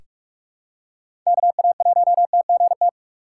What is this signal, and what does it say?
Morse code, heard as a keyed tone, about 46 wpm, 700 Hz. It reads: KA1TGT
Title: USA Call Signs - 6 Characters - 3x Character Spacing - 45wpm